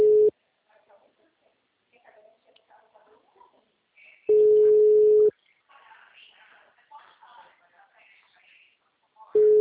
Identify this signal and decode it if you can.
Ringback tone (Europe, 425 Hz), with speech.